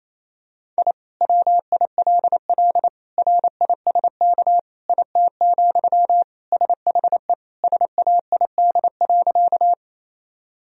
Morse code, heard as a keyed tone, about 28 wpm, 700 Hz. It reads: I WILL RISK IT, SHE SAID.
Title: Sentences from Top 1000 Words - Review Part 4 - 28wpm